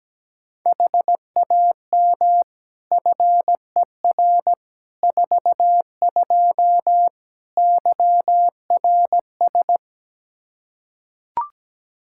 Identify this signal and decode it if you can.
Morse code, heard as a keyed tone, about 17 wpm, 700 Hz. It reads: HAM FER 42 YRS
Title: Practice QSO - Exchange 3 - More Info - 17wpm